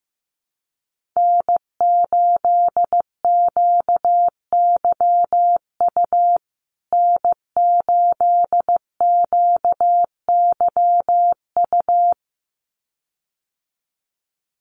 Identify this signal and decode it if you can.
Morse code, heard as a keyed tone, about 15 wpm, 700 Hz. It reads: N8QYU N8QYU
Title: USA Call Signs Repeated 2x 15wpm